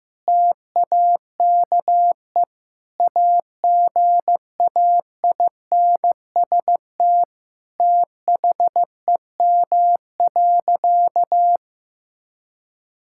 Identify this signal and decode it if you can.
Morse code, heard as a keyed tone, about 15 wpm, 700 Hz. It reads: TAKE AGAINST THEM.
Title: Sentences from Top 1000 Words - Review Part 6 - 15wpm